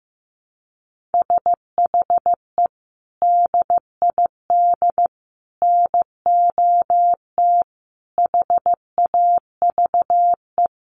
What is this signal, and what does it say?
Morse code, heard as a keyed tone, about 15 wpm, 700 Hz. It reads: SHE DID NOT HAVE
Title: Sets of 5 Words Encore2 15wpm